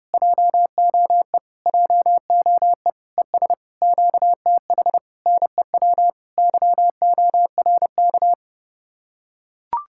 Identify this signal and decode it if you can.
Morse code, heard as a keyed tone, about 30 wpm, 700 Hz. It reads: JOE JOE ES QTH NEW YORK